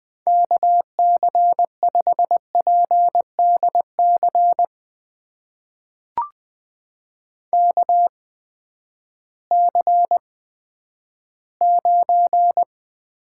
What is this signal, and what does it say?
Morse code, heard as a keyed tone, about 20 wpm, spaced as for 11 wpm, 700 Hz. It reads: KC5PDC KC9
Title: USA Call Signs - 6 Characters - 8x Character Spacing - 20wpm